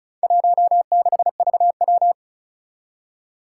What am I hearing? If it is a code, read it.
Morse code, heard as a keyed tone, about 36 wpm, 700 Hz. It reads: 16VW
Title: Four Challenging Letter-Number Combinations 35wpm